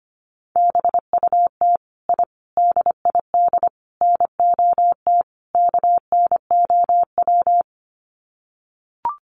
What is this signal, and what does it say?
Morse code, heard as a keyed tone, about 25 wpm, 700 Hz. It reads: BUT I DID NOT KNOW
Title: Sets of 5 Words Encore 25wpm